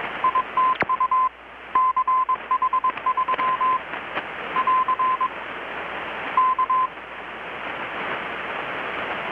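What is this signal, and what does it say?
Morse code, heard as a keyed tone, about 21 wpm, 1030 Hz, with radio noise. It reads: ITU CH3 <AR> K
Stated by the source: Morse from KPH's Night of Nights XX maritime radiotelegraphy event, received in Chiapas, Mexico about 3,800 km away on a Kenwood R-600 with a grounded 10 m long-wire antenna.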